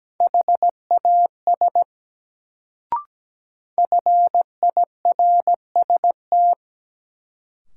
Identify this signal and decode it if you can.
Morse code, heard as a keyed tone, about 17 wpm, 700 Hz. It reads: HAS FIRST
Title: Sentences from Top 100 Words - New Word Drill - 17wpm